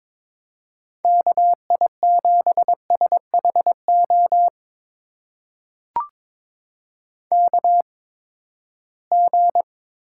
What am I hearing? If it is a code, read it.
Morse code, heard as a keyed tone, about 22 wpm, spaced as for 15 wpm, 700 Hz. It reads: KI7SHO KG
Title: USA Call Signs - 6 Characters - 8x Character Spacing - 22wpm